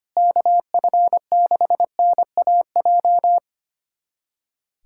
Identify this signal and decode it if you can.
Morse code, heard as a keyed tone, about 25 wpm, 700 Hz. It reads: KF6NAJ
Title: USA Callsign DE Callsign - 6 Characters - 25wpm